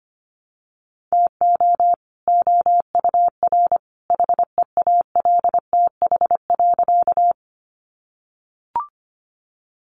Morse code, heard as a keyed tone, about 25 wpm, 700 Hz. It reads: TO OUR HEALTH.
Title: Sentences from Top 600 Words 25wpm